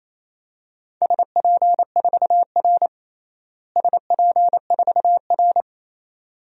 Morse code, heard as a keyed tone, about 28 wpm, 700 Hz. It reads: SP4R SP4R
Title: DXCC-100 Call Signs - 4 Characters - Repeated 2x - 28wpm